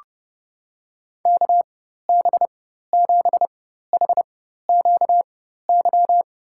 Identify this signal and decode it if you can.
Morse code, heard as a keyed tone, about 30 wpm, spaced as for 14 wpm, 700 Hz. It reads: KB7HQY
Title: USA Call Signs - 4x Character Spacing - 30wpm